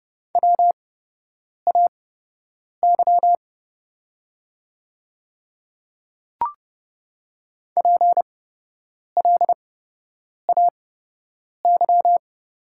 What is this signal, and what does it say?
Morse code, heard as a keyed tone, about 30 wpm, spaced as for 8 wpm, 700 Hz. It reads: WAY PLAY
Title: 8x Character Spacing - Top 500 Words - 30wpm